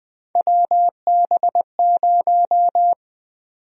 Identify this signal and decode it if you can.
Morse code, heard as a keyed tone, about 21 wpm, 700 Hz. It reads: WB0
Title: USA Call Sign Prefixes - 20wpm